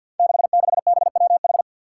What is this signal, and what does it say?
Morse code, heard as a keyed tone, about 53 wpm, 700 Hz. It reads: 66BWH
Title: Five Challenging Letter-Number Combinations 50wpm